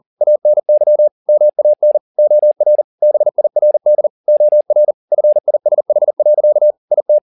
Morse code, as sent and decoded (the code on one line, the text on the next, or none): .- -. -.-- / -- .- -. / --- .-. / -... .. .-. -.. / --- .-. / ..-. .. ... .... .-.-.- / .. -
ANY MAN OR BIRD OR FISH. IT